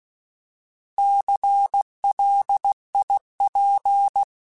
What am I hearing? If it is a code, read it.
Morse code, heard as a keyed tone, about 16 wpm, 800 Hz. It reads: CLIP
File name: Alphabetical / Alphabetical mp3s / Cs @15